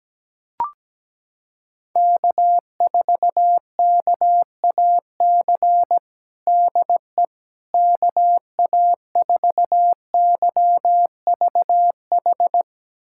Morse code, heard as a keyed tone, about 17 wpm, 700 Hz. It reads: K4KAC DE KA4YVH
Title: USA Callsign DE Callsign 17wpm